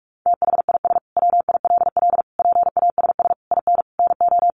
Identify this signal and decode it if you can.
Morse code, heard as a keyed tone, about 44 wpm, 700 Hz. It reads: THIS WILL PASS IN NO
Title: Sentences from Top 400 Words 45wpm